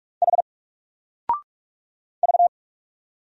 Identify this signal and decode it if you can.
Morse code, heard as a keyed tone, about 46 wpm, 700 Hz. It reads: H V
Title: Single Letter-Number 45wpm